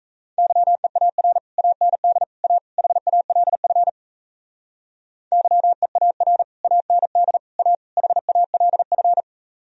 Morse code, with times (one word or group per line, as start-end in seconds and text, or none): 0.4-1.4 YEAR
1.6-2.2 AND
2.4-2.6 A
2.8-3.9 HALF
5.3-6.4 YEAR
6.6-7.4 AND
7.6-7.7 A
8.0-9.2 HALF